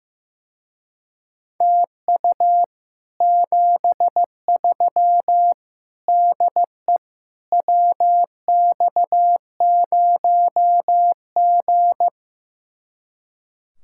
Morse code, heard as a keyed tone, about 15 wpm, 700 Hz. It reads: TU 73 DE WX0G
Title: Speed-Racing Common QSO Phrases Encore - 15wpm